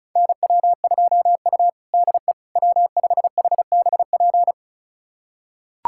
Morse code, heard as a keyed tone, about 36 wpm, 700 Hz. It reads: NW2U DE W5HBP
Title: USA Callsign DE Callsign 35wpm